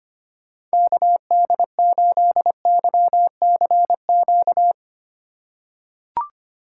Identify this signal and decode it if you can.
Morse code, heard as a keyed tone, about 25 wpm, 700 Hz. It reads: KD8YCQ
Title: USA Call Signs - 6 Characters - 2x Character Spacing - 25wpm